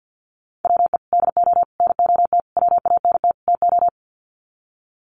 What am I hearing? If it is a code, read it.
Morse code, heard as a keyed tone, about 50 wpm, 700 Hz. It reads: WE DO NOT WANT TO
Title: Sets of 5 Words 50wpm